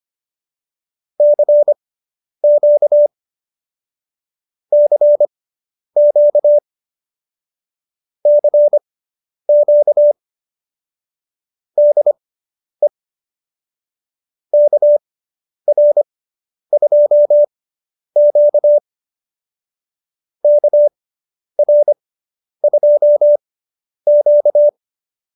Morse code, heard as a keyed tone, about 25 wpm, spaced as for 10 wpm, 590 Hz. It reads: CQ CQ CQ DE KR2Q KR2Q
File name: QSO220 10